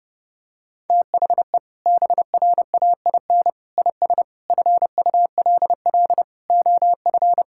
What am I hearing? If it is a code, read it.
Morse code, heard as a keyed tone, about 32 wpm, 700 Hz. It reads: THE BRAIN IS FULL OF
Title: Sentences from Top 1000 Words - Review Part 6 - 30wpm